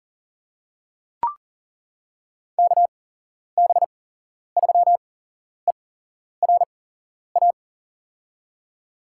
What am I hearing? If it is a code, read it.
Morse code, heard as a keyed tone, about 40 wpm, spaced as for 11 wpm, 700 Hz. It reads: KB3ERA
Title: USA Call Signs - 6 Characters - 8x Character Spacing - 40wpm